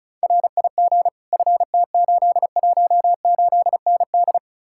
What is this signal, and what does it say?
Morse code, heard as a keyed tone, about 48 wpm, spaced as for 42 wpm, 700 Hz. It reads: RIG FT818ND
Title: Speed-Racing Common QSO Phrases - 35wpm